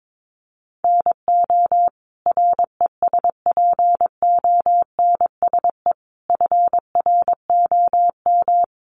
Morse code, heard as a keyed tone, about 22 wpm, 700 Hz. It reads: NO RESPONSE FROM
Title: Sentences from Top 1400 Words 22wpm